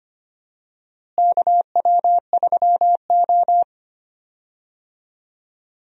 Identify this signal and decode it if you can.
Morse code, heard as a keyed tone, about 25 wpm, 700 Hz. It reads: KW3O
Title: USA Call Signs - 4 Characters - Repeated 3x - 25wpm